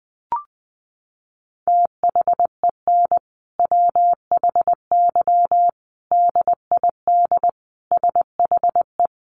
Morse code, heard as a keyed tone, about 20 wpm, 700 Hz. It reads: THEN WHY DID SHE
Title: Sentences from Top 800 Words 20wpm